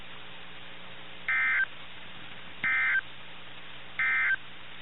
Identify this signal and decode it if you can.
Emergency Alert System (SAME) tones, heard as harsh buzzing bursts. It is heard from 1.2 to 4.4 s.
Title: NOAA Weather Radio - KIH61